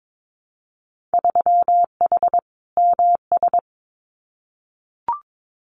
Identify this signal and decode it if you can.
Morse code, heard as a keyed tone, about 22 wpm, 700 Hz. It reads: 3H MS
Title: Winter Field Day Exchanges Encore 22wpm